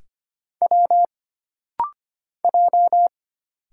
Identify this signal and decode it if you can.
Morse code, heard as a keyed tone, about 25 wpm, 700 Hz. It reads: W J
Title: Single Letters - Rapid-Fire 25wpm